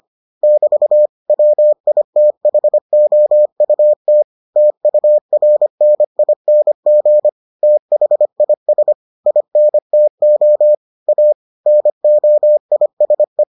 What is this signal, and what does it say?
Morse code, heard as a keyed tone, about 25 wpm, 600 Hz. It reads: <BT> WITHOUT TURNING THIS INTO A NOISE